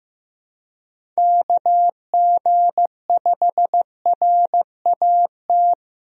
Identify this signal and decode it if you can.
Morse code, heard as a keyed tone, about 15 wpm, 700 Hz. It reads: KG5RAT
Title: USA Call Signs - 6 Characters - 15wpm